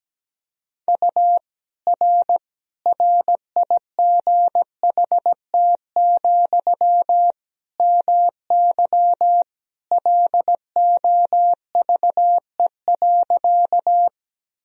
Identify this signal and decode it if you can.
Morse code, heard as a keyed tone, about 17 wpm, 700 Hz. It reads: U R RIGHT, MY LOVE.
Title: Sentences from Top 400 Words and Abbreviations 17wpm